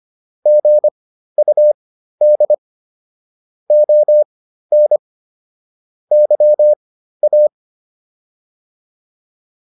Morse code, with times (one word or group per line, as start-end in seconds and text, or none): 0.5-2.5 GUD
3.7-5.0 ON
6.1-7.5 YA